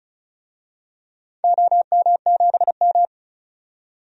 Morse code, heard as a keyed tone, about 34 wpm, 700 Hz. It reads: OM7M
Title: DXCC-100 Call Signs - 4 Characters - 35wpm